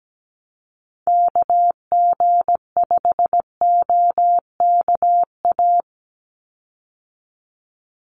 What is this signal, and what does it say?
Morse code, heard as a keyed tone, about 17 wpm, 700 Hz. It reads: KG5OKA
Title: USA Call Signs 17wpm